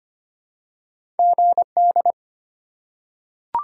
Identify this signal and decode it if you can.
Morse code, heard as a keyed tone, about 25 wpm, 700 Hz. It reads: GD
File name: Top 90 QSO Elements - 4x Character Spacing - 25wpm